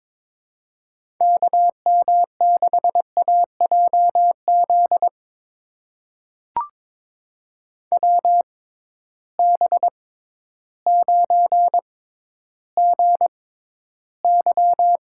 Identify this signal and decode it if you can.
Morse code, heard as a keyed tone, about 22 wpm, spaced as for 12 wpm, 700 Hz. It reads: KM6AJZ WB9GY
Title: USA Call Signs - 6x Character Spacing - 22wpm